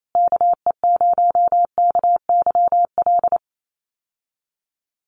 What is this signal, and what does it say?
Morse code, heard as a keyed tone, about 28 wpm, 700 Hz. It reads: KE0KYL
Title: USA Call Signs Repeated 3x 28wpm